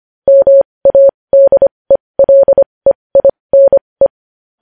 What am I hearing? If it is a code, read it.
Morse code, heard as a keyed tone, about 25 wpm, spaced as for 20 wpm, 550 Hz. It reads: MADELEINE
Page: News Headlines in Morse Code at 20 WPM